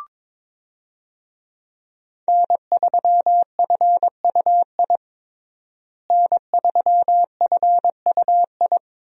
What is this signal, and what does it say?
Morse code, heard as a keyed tone, about 21 wpm, 700 Hz. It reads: N3FUI N3FUI
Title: USA Call Signs - 5 Characters - Repeated 2x - 22wpm